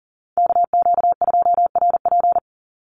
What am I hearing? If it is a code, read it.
Morse code, heard as a keyed tone, about 38 wpm, 700 Hz. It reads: KQ2RP
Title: NA SOTA Chaser Callsigns 40wpm